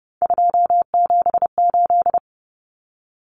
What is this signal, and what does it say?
Morse code, heard as a keyed tone, about 30 wpm, 700 Hz. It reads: 278
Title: Common Cut Numbers - 3-Digit 30wpm